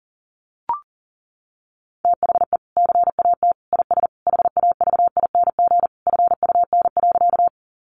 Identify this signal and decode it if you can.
Morse code, heard as a keyed tone, about 40 wpm, 700 Hz. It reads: THE CAT IS HAVING FUN.
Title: Easy Phrases 40wpm